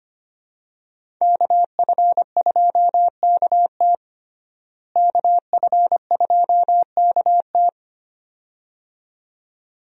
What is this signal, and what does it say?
Morse code, heard as a keyed tone, about 25 wpm, 700 Hz. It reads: KF2KT KF2KT
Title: USA Call Signs - 5 Characters - Repeated 2x - 25wpm